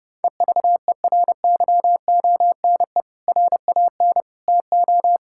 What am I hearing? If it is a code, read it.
Morse code, heard as a keyed tone, about 30 wpm, 700 Hz. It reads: EVERYONE RAN TO
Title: Sentences from Top 1600 Words 30wpm